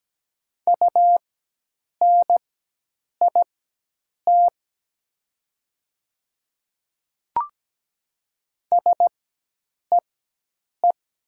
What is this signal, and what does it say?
Morse code, heard as a keyed tone, about 17 wpm, spaced as for 8 wpm, 700 Hz. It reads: UNIT SEE
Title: 4x Character Spacing - Top 500 Words - 17wpm